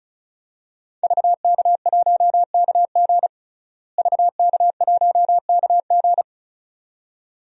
Morse code, heard as a keyed tone, about 34 wpm, 700 Hz. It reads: VK1KG VK1KG
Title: DXCC-100 Call Signs - 5 Characters - Repeated 2x - 35wpm